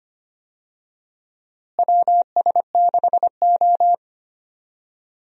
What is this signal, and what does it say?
Morse code, heard as a keyed tone, about 25 wpm, 700 Hz. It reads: WS6O
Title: USA Call Signs - 4 Characters - Repeated 3x - 25wpm